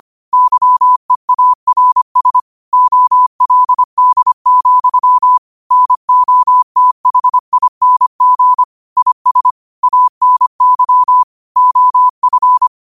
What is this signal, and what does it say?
Morse code, heard as a keyed tone, about 25 wpm, 1000 Hz. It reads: YEARS OLD, NOTHING IS ANY OF
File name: qotdfiles / 25wpm